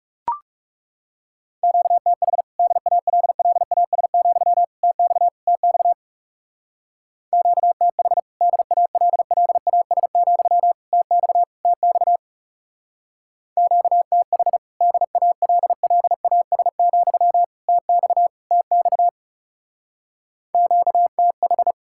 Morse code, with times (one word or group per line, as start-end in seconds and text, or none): 1.6-2.4 QTH
2.6-4.6 DALLAS,
4.8-5.3 TX
5.5-5.9 TX
7.3-8.2 QTH
8.4-10.7 DALLAS,
10.9-11.4 TX
11.7-12.2 TX
13.6-14.6 QTH
14.8-17.4 DALLAS,
17.7-18.3 TX
18.5-19.1 TX
20.5-21.7 QTH